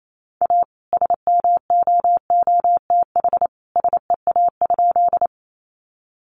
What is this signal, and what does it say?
Morse code, heard as a keyed tone, about 28 wpm, 700 Hz. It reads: A SMOOTH SEA?
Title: Sentences from Top 1400 Words 28wpm